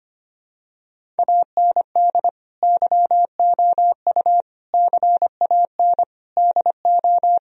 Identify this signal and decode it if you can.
Morse code, heard as a keyed tone, about 25 wpm, 700 Hz. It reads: AND YOU CAN DO
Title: Sentences from Top 1000 Words - Review Part 1 - 25wpm